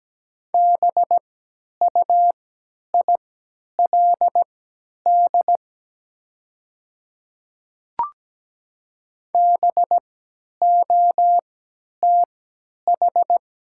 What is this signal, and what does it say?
Morse code, heard as a keyed tone, about 17 wpm, spaced as for 10 wpm, 700 Hz. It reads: BUILD BOTH
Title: 3x Character Spacing - Top 300 Words - 17wpm